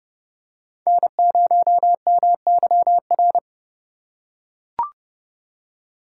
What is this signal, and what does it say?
Morse code, heard as a keyed tone, about 29 wpm, 700 Hz. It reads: N0MYR